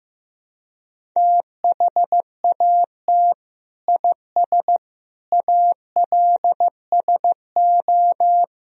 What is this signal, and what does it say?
Morse code, heard as a keyed tone, about 15 wpm, 700 Hz. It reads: THAT IS ALSO
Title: Sentences from Top 1000 Words - Review Part 1 - 15wpm